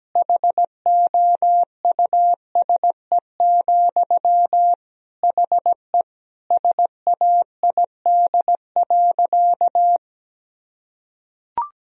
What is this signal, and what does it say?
Morse code, heard as a keyed tone, about 17 wpm, 700 Hz. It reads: HOUSE, HE SAID.
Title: Sentences from Top 1000 Words - Review Part 1 - 17wpm